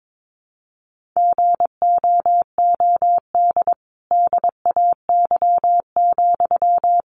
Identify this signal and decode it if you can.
Morse code, heard as a keyed tone, about 22 wpm, 700 Hz. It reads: GOOD DAY,
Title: Sentences from Top 1400 Words 22wpm